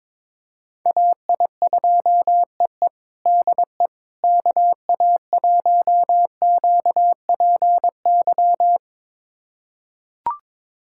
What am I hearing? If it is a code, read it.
Morse code, heard as a keyed tone, about 22 wpm, 700 Hz. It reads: AI2EE DE KA1QPY
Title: USA Callsign DE Callsign 22wpm